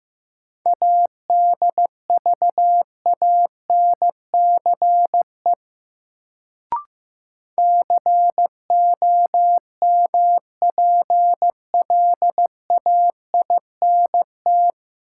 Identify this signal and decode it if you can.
Morse code, heard as a keyed tone, about 16 wpm, 700 Hz. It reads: ADVANCE COMPLAINT
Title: Sentences from Top 1400 Words - New Word Drill - 15wpm